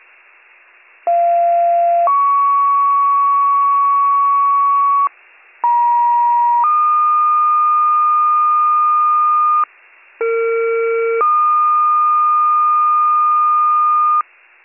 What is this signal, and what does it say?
Two-tone page, heard as a one-second tone, then a three-second tone, about 690 then 1120 Hz, about 950 then 1220 Hz, about 460 then 1220 Hz. It is heard from 1.1 to 14.2 s.